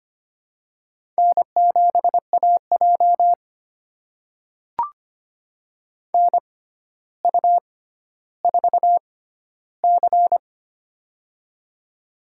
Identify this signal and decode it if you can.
Morse code, heard as a keyed tone, about 25 wpm, spaced as for 13 wpm, 700 Hz. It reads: N7AJ NU4C
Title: USA Call Signs - 4 Characters - 6x Character Spacing - 25wpm